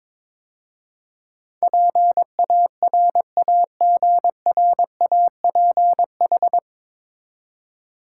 Morse code, heard as a keyed tone, about 22 wpm, 700 Hz. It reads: PARAGRAPH
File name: 9 Letter Words - First 3 Letters Given - 22wpm